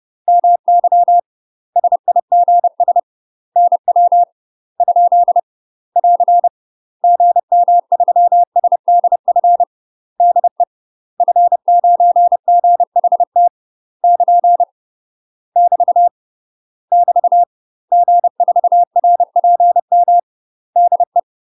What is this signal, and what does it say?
Morse code, heard as a keyed tone, about 30 wpm, 700 Hz. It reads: MY SIGS NW ? <AR> GM3SDF DE F9GHT <KN> <BT> <BT> G4RPM DE
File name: UK MASTER QSOs TRAINING / 30WPM / UK MASTER QSOs 0021